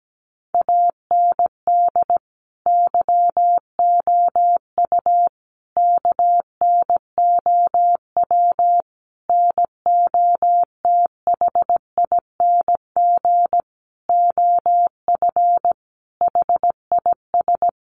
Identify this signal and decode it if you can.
Morse code, heard as a keyed tone, about 17 wpm, 700 Hz. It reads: AND YOU KNOW NOTHING OF HIS
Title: Sentences from Top 1200 Words 17wpm